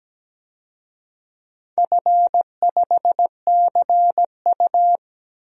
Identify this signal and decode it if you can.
Morse code, heard as a keyed tone, about 17 wpm, 700 Hz. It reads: F5CU
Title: DXCC-100 Call Signs - 4 Characters - 17wpm